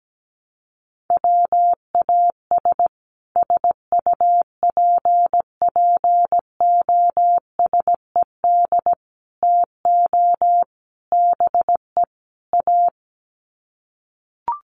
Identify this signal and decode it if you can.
Morse code, heard as a keyed tone, about 17 wpm, 700 Hz. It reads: WAS SUPPOSED TO BE A
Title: Sets of 5 Words Encore 17wpm